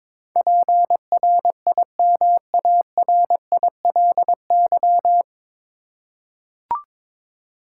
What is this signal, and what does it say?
Morse code, heard as a keyed tone, about 22 wpm, 700 Hz. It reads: PRIMARILY
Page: Morse Code Ninja practice 22wpm